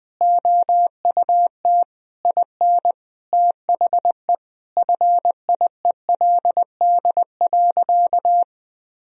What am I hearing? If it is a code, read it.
Morse code, heard as a keyed tone, about 20 wpm, 700 Hz. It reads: OUT IN THE FIELD.
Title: Sentences from Top 1000 Words - Review Part 4 - 20wpm